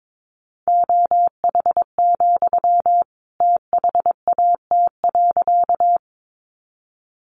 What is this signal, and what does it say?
Morse code, heard as a keyed tone, about 22 wpm, 700 Hz. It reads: OH, THAT.
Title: Sentences from Top 500 Words 22wpm